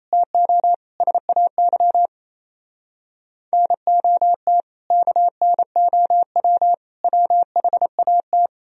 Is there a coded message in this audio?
Yes: Morse code from 0.1 to 8.5 s, heard as a keyed tone, about 38 wpm, 700 Hz. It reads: TO SAY NOT KNOW WHAT